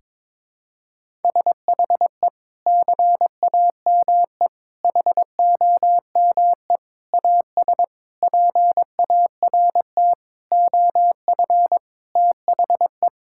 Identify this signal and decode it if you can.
Morse code, heard as a keyed tone, about 21 wpm, 700 Hz. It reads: SHE CAME HOME AS PART OF THE
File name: practice / 22wpm / Sentences with 4-Word Phrases - 22wpm